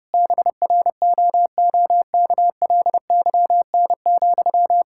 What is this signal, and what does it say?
Morse code, heard as a keyed tone, about 32 wpm, 700 Hz. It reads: BROOKLYN,
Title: Practice QSO - Exchange 1 - Beginning - 30wpm